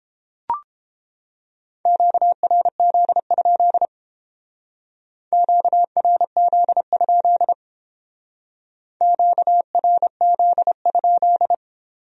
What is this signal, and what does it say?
Morse code, heard as a keyed tone, about 29 wpm, 700 Hz. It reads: QRZ? QRZ? QRZ?